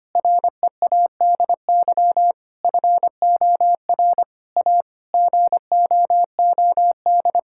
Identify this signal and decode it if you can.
Morse code, heard as a keyed tone, about 25 wpm, 700 Hz. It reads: READY FOR A GOOD
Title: Sentences from Top 1000 Words - Review Part 5 - 25wpm